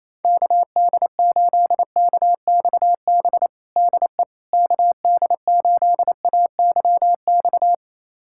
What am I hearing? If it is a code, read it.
Morse code, heard as a keyed tone, about 28 wpm, 700 Hz. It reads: KD8KXB DE KD8AYX
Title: USA Callsign DE Callsign 28wpm